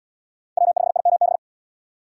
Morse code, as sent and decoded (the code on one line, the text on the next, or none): .-- ..... .. -- -...
W5IMB